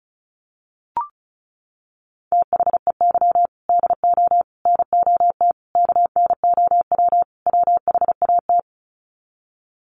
Morse code, heard as a keyed tone, about 34 wpm, 700 Hz. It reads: THEY DO NOT KNOW WHAT
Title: Sets of 5 Words Encore 35wpm